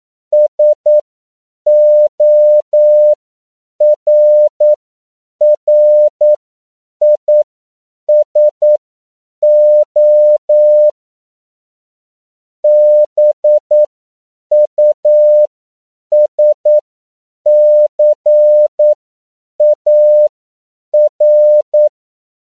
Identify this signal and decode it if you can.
Morse code, heard as a keyed tone, about 9 wpm, spaced as for 7 wpm, 600 Hz. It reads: SORRISO BUSCAR
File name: CW / 10PPM